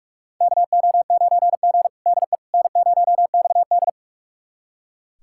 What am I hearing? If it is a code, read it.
Morse code, heard as a keyed tone, about 41 wpm, 700 Hz. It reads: KO9G DE N0XD
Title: USA Callsign DE Callsign - 4 Characters - 45wpm